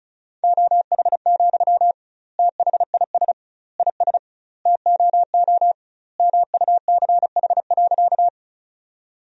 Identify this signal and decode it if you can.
Morse code, heard as a keyed tone, about 34 wpm, 700 Hz. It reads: OH, THIS IS TOO MUCH.
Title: 2x Word-Spacing - Sentences from Top 500 Words - 35wpm